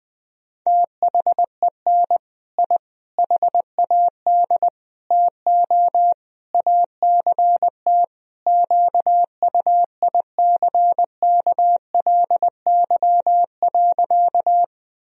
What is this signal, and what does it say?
Morse code, heard as a keyed tone, about 21 wpm, 700 Hz. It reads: THEN I HAD TO ACT QUICKLY.
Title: Sentences from Top 1000 Words - Review Part 1 - 20wpm